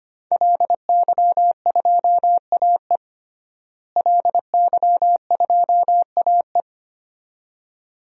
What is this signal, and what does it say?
Morse code, heard as a keyed tone, about 25 wpm, 700 Hz. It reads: LY2AE LY2AE
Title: DXCC-100 Call Signs - 5 Characters - Repeated 2x - 25wpm